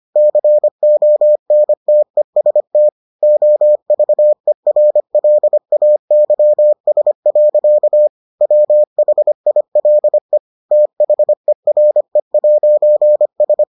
Morse code, heard as a keyed tone, about 25 wpm, 600 Hz. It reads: CONTEST OVERLAYS. WHILE THERE'S